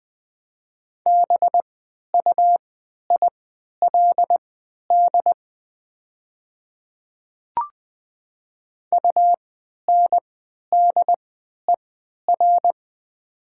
Morse code, heard as a keyed tone, about 20 wpm, spaced as for 11 wpm, 700 Hz. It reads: BUILD UNDER
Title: 3x Character Spacing - Top 200 Words - 20wpm